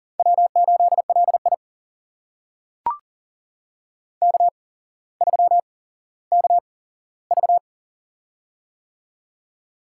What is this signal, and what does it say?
Morse code, heard as a keyed tone, about 40 wpm, spaced as for 17 wpm, 700 Hz. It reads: W8LI K3KV
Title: USA Call Signs - 8x Character Spacing - 40wpm